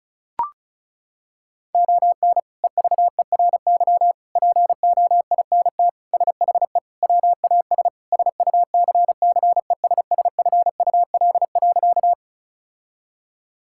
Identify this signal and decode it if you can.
Morse code, heard as a keyed tone, about 34 wpm, 700 Hz. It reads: ON EVERY POINT SHE WAS SUCCESSFUL.
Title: Sentences from Top 1000 Words - Review Part 6 - 35wpm